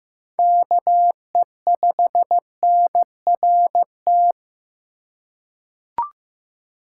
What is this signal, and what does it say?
Morse code, heard as a keyed tone, about 15 wpm, 700 Hz. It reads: KE5NRT
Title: USA Call Signs - 3x Character Spacing - 15wpm